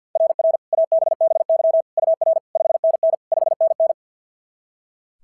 Morse code, heard as a keyed tone, about 44 wpm, 660 Hz. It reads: RR ABBY UR 5NN 5NN
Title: SOTA QSO Practice 50wpm